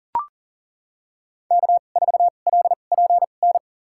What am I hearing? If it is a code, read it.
Morse code, heard as a keyed tone, about 39 wpm, spaced as for 29 wpm, 700 Hz. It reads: K4LPN
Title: USA Call Signs - 2x Character Spacing - 40wpm